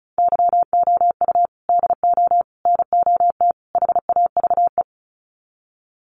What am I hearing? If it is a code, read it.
Morse code, heard as a keyed tone, about 34 wpm, 700 Hz. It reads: YOU DO NOT HAVE
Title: Sets of 4 Words 35wpm